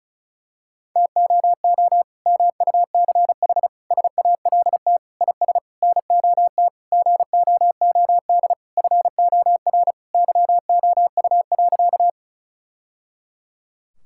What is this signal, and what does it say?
Morse code, heard as a keyed tone, about 34 wpm, 700 Hz. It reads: TOO MUCH SALT IS NOT GOOD FOR YOU.
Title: Sentences from Top 1000 Words - Review Part 5 - 35wpm